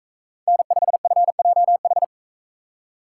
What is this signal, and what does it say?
Morse code, heard as a keyed tone, about 42 wpm, 700 Hz. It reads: N5FJH